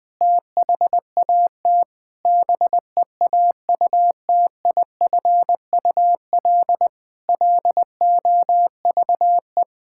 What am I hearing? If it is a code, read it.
Morse code, heard as a keyed tone, about 21 wpm, 700 Hz. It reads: THAT BEAUTIFUL LOVE
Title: Sentences from Top 1600 Words 20wpm